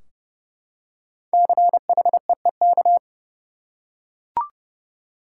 Morse code, heard as a keyed tone, about 31 wpm, 700 Hz. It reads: CHEEK
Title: Five Letter Words - 30wpm